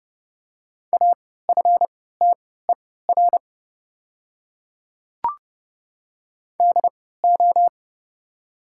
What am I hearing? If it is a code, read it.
Morse code, heard as a keyed tone, about 30 wpm, spaced as for 17 wpm, 700 Hz. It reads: AFTER DO
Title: 3x Character Spacing - Top 500 Words - 30wpm